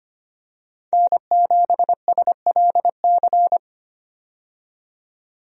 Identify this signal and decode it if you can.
Morse code, heard as a keyed tone, about 25 wpm, 690 Hz. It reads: N7SLC
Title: USA Call Signs - 5 Characters - 25wpm